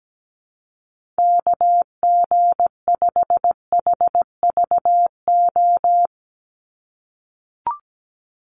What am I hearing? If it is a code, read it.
Morse code, heard as a keyed tone, about 17 wpm, 700 Hz. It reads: KG5HVO
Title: Call Signs - 17wpm